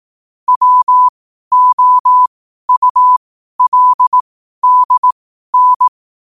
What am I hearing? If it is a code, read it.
Morse code, heard as a keyed tone, about 18 wpm, spaced as for 12 wpm, 1000 Hz. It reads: WOULDN
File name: qotdfiles / 12wpm